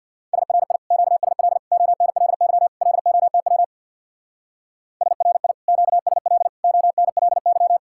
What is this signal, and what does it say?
Morse code, heard as a keyed tone, about 60 wpm, 700 Hz. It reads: SRI QSL ONLY LOTW SRI QSL ONLY
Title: Speed-Racing Common QSO Phrases - 50wpm